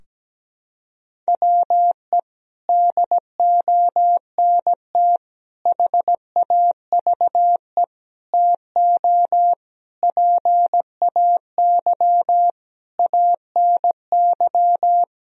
Morse code, heard as a keyed tone, about 17 wpm, 700 Hz. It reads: WE DONT HAVE TO PAY ANY
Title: Sentences from Top 1000 Words - Review Part 2 - 17wpm